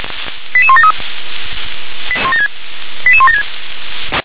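Five-tone selective call (ZVEI), heard as quick runs of five tones, tones of about 70 ms. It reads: 88173 88176
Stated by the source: ZVEI five-tone selective call 88170 received via SDR.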